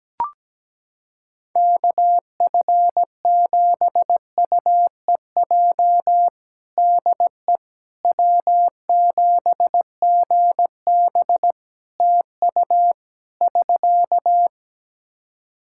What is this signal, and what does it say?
Morse code, heard as a keyed tone, about 17 wpm, 700 Hz. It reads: KF7UEJ DE W7GB TU <SK>
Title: Practice QSO - Exchange 4 - Ending - 17wpm